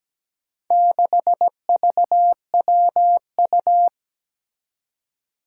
Morse code, heard as a keyed tone, about 17 wpm, 700 Hz. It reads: 6VWU
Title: Four Challenging Letter-Number Combinations 17wpm